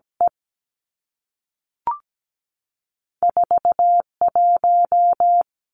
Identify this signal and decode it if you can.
Morse code, heard as a keyed tone, about 17 wpm, 700 Hz. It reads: E 4 1
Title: Two Challenging Letter-Number Combinations 17wpm